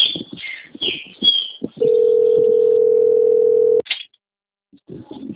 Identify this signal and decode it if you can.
Ringback tone (North America), 440 + 480 Hz.
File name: out- - - - - .1914